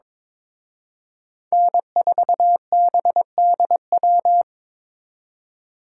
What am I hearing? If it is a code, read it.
Morse code, heard as a keyed tone, about 26 wpm, 700 Hz. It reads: N4BDW
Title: Speed Racing - USA Call Signs 22wpm